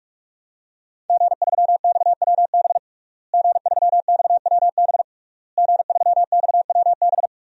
Morse code, heard as a keyed tone, about 41 wpm, 700 Hz. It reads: G3XWB G3XWB G3XWB
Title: DXCC-100 Call Signs - 5 Characters - Repeated 3x - 45wpm